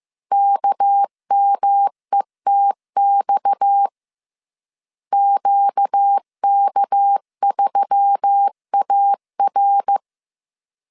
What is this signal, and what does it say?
Morse code, heard as a keyed tone, about 15 wpm, 800 Hz. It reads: KMETX QK3AR
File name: CW / DET / 1 6 1500C S C 14wpm 2951